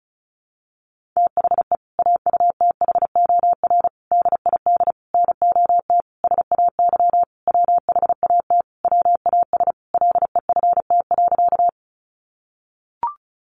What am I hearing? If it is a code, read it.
Morse code, heard as a keyed tone, about 34 wpm, 700 Hz. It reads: THE AUTHOR DID NOT SAY WHAT WAS LEFT.
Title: Sentences from Top 1200 Words 35wpm